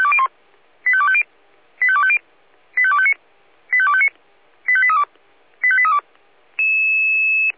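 Five-tone selective call (ZVEI), heard as quick runs of five tones, tones of about 70 ms. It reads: 85250 85250 85259 85259 85832 85832
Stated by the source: ZVEI five-tone call received by SDR.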